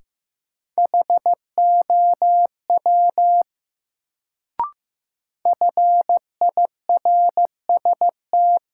Morse code, heard as a keyed tone, about 15 wpm, 700 Hz. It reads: HOW FIRST
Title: Sentences from Top 100 Words - New Word Drill - 15wpm